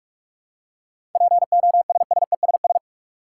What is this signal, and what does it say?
Morse code, heard as a keyed tone, about 46 wpm, 700 Hz. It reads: POSSESS
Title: Seven Letter Words - 45wpm